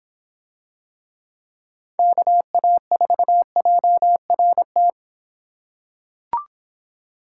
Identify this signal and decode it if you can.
Morse code, heard as a keyed tone, about 26 wpm, 700 Hz. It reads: KA4JRT